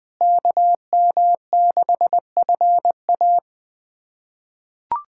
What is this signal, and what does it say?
Morse code, heard as a keyed tone, about 20 wpm, 700 Hz. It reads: KM6FA
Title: USA Callsign DE Callsign - 5 Characters - 20wpm